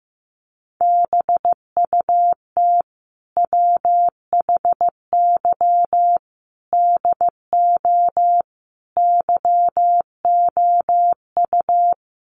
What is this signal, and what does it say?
Morse code, heard as a keyed tone, about 15 wpm, 700 Hz. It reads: BUT WHY DO YOU
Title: Sentences from Top 200 Words 15wpm